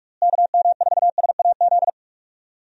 Morse code, heard as a keyed tone, about 46 wpm, 700 Hz. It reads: KM4SAZ
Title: USA Call Signs - 6 Characters - Repeated 2x - 45wpm